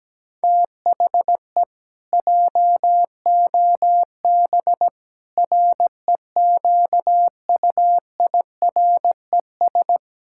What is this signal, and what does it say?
Morse code, heard as a keyed tone, about 17 wpm, 700 Hz. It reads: THE JOB REQUIRES
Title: Pangram Challenge v2 - 17wpm